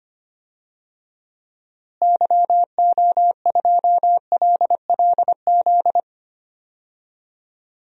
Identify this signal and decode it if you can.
Morse code, heard as a keyed tone, about 25 wpm, 700 Hz. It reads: YO2LLZ
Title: DXCC-100 Call Signs - 6 Characters - Repeated 2x - 25wpm